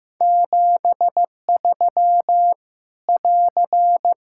Morse code, heard as a keyed tone, about 15 wpm, 700 Hz. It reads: 73 <AR>
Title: Practice QSO - Exchange 4 - Ending - 15wpm